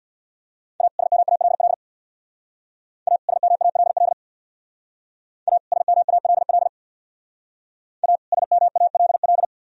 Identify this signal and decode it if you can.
Morse code, heard as a keyed tone, about 65 wpm, 700 Hz. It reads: A SMALL A SMALL A SMALL A SMALL